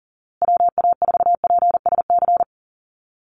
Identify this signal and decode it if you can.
Morse code, heard as a keyed tone, about 40 wpm, 700 Hz. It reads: WA4PSC
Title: Call Signs Repeated 2x - 40wpm